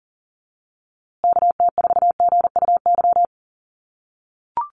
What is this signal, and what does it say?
Morse code, heard as a keyed tone, about 40 wpm, 700 Hz. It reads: KT4GUY
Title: USA Call Signs Repeated 2x 40wpm